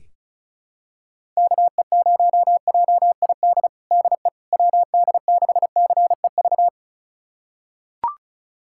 Morse code, heard as a keyed tone, about 36 wpm, 700 Hz. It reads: KE0JID DE WD6CEV
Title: USA Callsign DE Callsign - 6 Characters - 35wpm